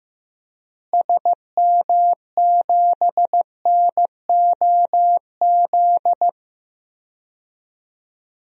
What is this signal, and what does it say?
Morse code, heard as a keyed tone, about 15 wpm, 700 Hz. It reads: SM7NOZ
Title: DXCC-100 Call Signs - 6 Characters - Repeated 3x - 15wpm